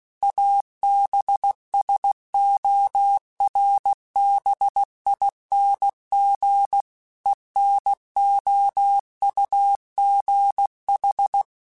Morse code, heard as a keyed tone, about 16 wpm, 800 Hz. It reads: ABSORBING ENOUGH
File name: Technical Subjects / Technical Subjects-mp3s / XC @15